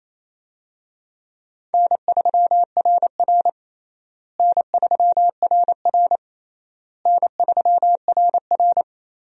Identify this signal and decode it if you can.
Morse code, heard as a keyed tone, about 27 wpm, 700 Hz. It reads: N3RR N3RR N3RR
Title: USA Call Signs - 4 Characters - Repeated 3x - 28wpm